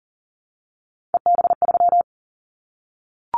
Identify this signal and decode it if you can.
Morse code, heard as a keyed tone, about 40 wpm, 700 Hz. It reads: EB3
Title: Callsign Prefixes for Top 25 DXCC - 40wpm